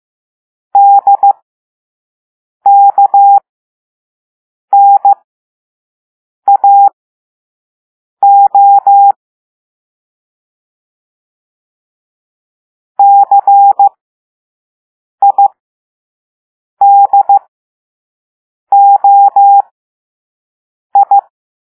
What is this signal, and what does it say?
Morse code, heard as a keyed tone, about 15 wpm, spaced as for 5 wpm, 800 Hz. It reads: DKNAO CIDOI
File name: Morse / Level6-All-A